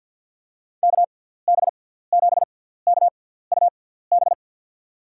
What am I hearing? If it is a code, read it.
Morse code, heard as a keyed tone, about 53 wpm, spaced as for 17 wpm, 700 Hz. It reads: KB7KUB
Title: USA Call Signs - 6 Characters - 6x Character Spacing - 50wpm